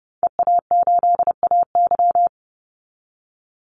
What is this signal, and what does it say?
Morse code, heard as a keyed tone, about 30 wpm, 700 Hz. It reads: EA8AY
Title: Call Signs - 30wpm